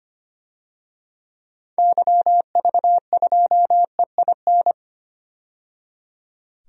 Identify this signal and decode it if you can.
Morse code, heard as a keyed tone, about 25 wpm, 700 Hz. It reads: YV2EIN
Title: DXCC-100 Call Signs - 6 Characters - 25wpm